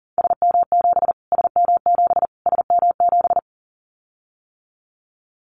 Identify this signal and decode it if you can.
Morse code, heard as a keyed tone, about 40 wpm, 700 Hz. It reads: SM7 SM7 SM7
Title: Callsign Prefixes for Top 25 DXCC - Repeated 3x - 40wpm